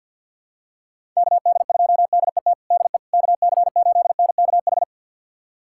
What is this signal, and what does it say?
Morse code, heard as a keyed tone, about 46 wpm, 700 Hz. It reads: KD2DET DE KC8NKH
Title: USA Callsign DE Callsign - 6 Characters - 50wpm